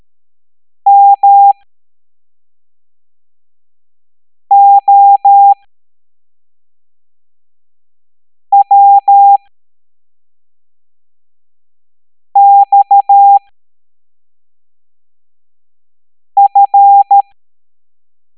Morse code, heard as a keyed tone, about 13 wpm, spaced as for 3 wpm, 800 Hz. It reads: MOWXF